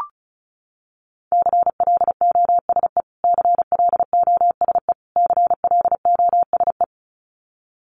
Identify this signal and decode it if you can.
Morse code, heard as a keyed tone, about 34 wpm, 700 Hz. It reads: CLOSE CLOSE CLOSE
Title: Top 300 Words Repeated 35wpm